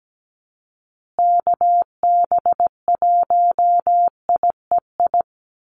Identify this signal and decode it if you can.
Morse code, heard as a keyed tone, about 17 wpm, 700 Hz. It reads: KB1IEI
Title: USA Call Signs Repeated 2x 17wpm